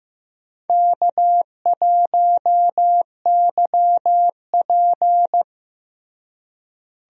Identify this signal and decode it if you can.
Morse code, heard as a keyed tone, about 15 wpm, 700 Hz. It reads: K1YP
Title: USA Call Signs - 4 Characters - Repeated 3x - 15wpm